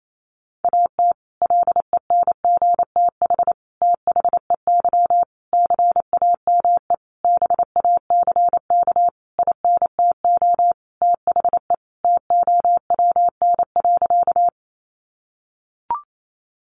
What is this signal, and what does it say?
Morse code, heard as a keyed tone, about 28 wpm, 700 Hz. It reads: AT LENGTH THEY CAME BACK INTO THE TOWN.
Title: Sentences from Top 1000 Words 28wpm